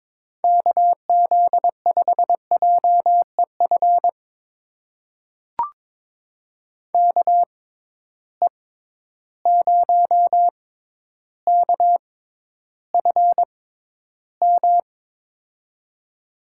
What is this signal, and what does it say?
Morse code, heard as a keyed tone, about 22 wpm, spaced as for 11 wpm, 700 Hz. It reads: KZ5JEF KE0KFM
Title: USA Call Signs - 6 Characters - 6x Character Spacing - 22wpm